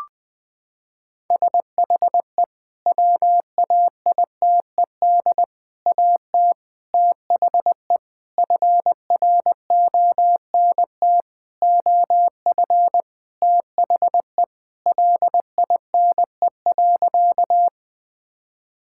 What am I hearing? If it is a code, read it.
Morse code, heard as a keyed tone, about 19 wpm, 700 Hz. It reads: SHE WAITED AT THE FRONT OF THE LINE.
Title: Sentences with 5-Word Phrases - 20wpm